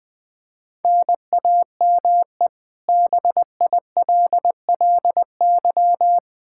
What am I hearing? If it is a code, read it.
Morse code, heard as a keyed tone, about 20 wpm, 700 Hz. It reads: NAME BILLY
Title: Practice QSO - Exchange 1 - Beginning - 20wpm